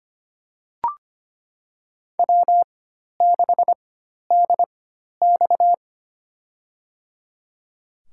Morse code, heard as a keyed tone, about 25 wpm, spaced as for 12 wpm, 700 Hz. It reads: W6DX
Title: USA Call Signs - 4 Characters - 4x Character Spacing - 25wpm